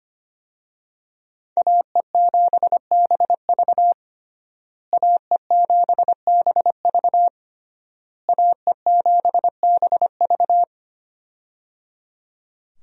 Morse code, heard as a keyed tone, about 25 wpm, 700 Hz. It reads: AE7BV AE7BV AE7BV
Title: USA Call Signs - 5 Characters - Repeated 3x - 25wpm